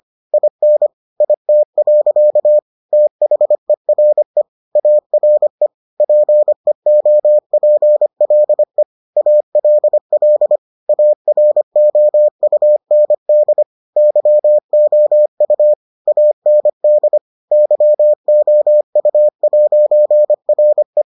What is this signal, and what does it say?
Morse code, heard as a keyed tone, about 25 wpm, 600 Hz. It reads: IN IT. THERE ARE PEOPLE ALL AROUND YOU AND YOU'RE